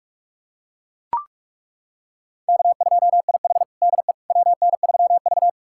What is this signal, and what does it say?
Morse code, heard as a keyed tone, about 41 wpm, 700 Hz. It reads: K2IH DE WN3V
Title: USA Callsign DE Callsign - 4 Characters - 45wpm